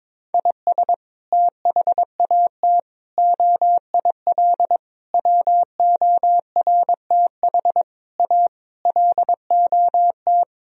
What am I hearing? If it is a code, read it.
Morse code, heard as a keyed tone, about 21 wpm, 700 Hz. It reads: IS THAT OIL WORTH A LOT
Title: Sentences from Top 1000 Words - Review Part 4 - 22wpm